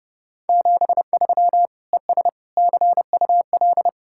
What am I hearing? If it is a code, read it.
Morse code, heard as a keyed tone, about 35 wpm, 700 Hz. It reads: 73 ES CUL
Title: Speed-Racing Common QSO Phrases - 30wpm